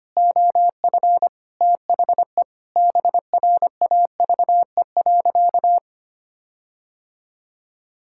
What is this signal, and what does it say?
Morse code, heard as a keyed tone, about 25 wpm, 700 Hz. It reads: OF THE BRAVE.
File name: practice / 25wpm / Sentences with 9-Word Phrases - 25wpm